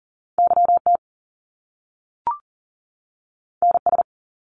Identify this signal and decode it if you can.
Morse code, heard as a keyed tone, about 40 wpm, 700 Hz. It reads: YT NS
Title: Canadian Provinces and Territories 40wpm